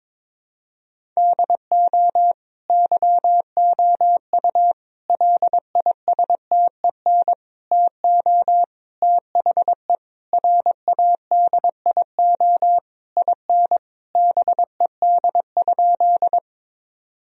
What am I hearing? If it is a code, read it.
Morse code, heard as a keyed tone, about 22 wpm, 700 Hz. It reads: DO YOU LISTEN TO THE RADIO IN BED?
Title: Sentences from Top 1000 Words - Review Part 3 - 22wpm